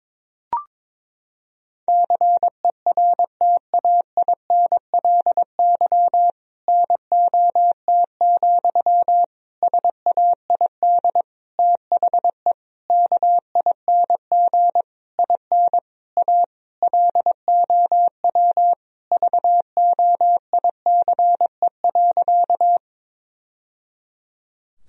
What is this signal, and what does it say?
Morse code, heard as a keyed tone, about 22 wpm, 700 Hz. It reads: CERTAINLY NOT, SAID THE KING IN A LOW VOICE.
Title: Sentences from Top 1000 Words - Review Part 1 - 22wpm